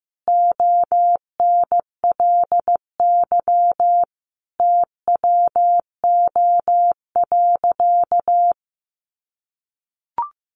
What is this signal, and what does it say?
Morse code, heard as a keyed tone, about 15 wpm, 700 Hz. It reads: ONLY TWO.
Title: Sentences from Top 200 Words 15wpm